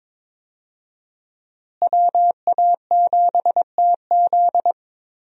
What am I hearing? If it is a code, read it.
Morse code, heard as a keyed tone, about 22 wpm, 700 Hz. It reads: WA7TZ
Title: USA Call Signs - 5 Characters - Repeated 2x - 22wpm